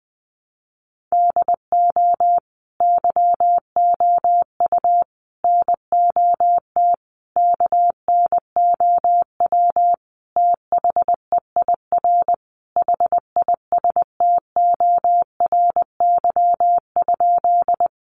Morse code, heard as a keyed tone, about 20 wpm, 700 Hz. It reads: DO YOU NOT KNOW THEIR HISTORY?
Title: Sentences from Top 600 Words 20wpm